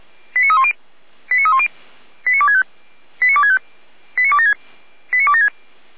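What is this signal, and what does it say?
Five-tone selective call (ZVEI), heard as quick runs of five tones, tones of about 70 ms. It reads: 89310 89310 89265 89265 89276 89276
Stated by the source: ZVEI five-tone selective call received by SDR.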